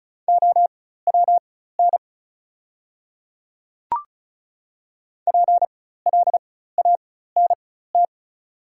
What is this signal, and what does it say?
Morse code, heard as a keyed tone, about 36 wpm, spaced as for 16 wpm, 690 Hz. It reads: OWN PLANT
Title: 4x Character Spacing - Top 500 Words - 35wpm